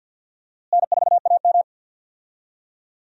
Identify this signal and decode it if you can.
Morse code, heard as a keyed tone, about 52 wpm, 700 Hz. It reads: N4AM
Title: USA Call Signs - 4 Characters - Repeated 3x - 50wpm